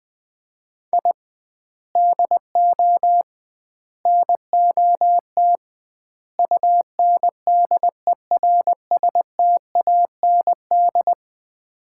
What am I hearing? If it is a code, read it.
Morse code, heard as a keyed tone, about 21 wpm, 700 Hz. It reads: I DO NOT UNDERSTAND
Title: 2x Word-Spacing - Sets of 4 Words - 20wpm